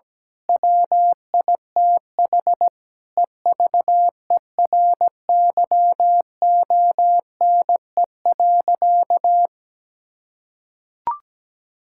Morse code, heard as a keyed tone, about 17 wpm, 700 Hz. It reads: WITH EVERYONE.
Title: Sentences from Top 1000 Words - Review Part 5 - 17wpm